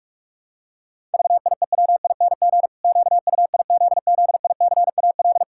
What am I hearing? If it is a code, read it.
Morse code, heard as a keyed tone, about 41 wpm, 700 Hz. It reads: VIEWING QUIZZICAL
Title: Pangram Challenge v2 - 45wpm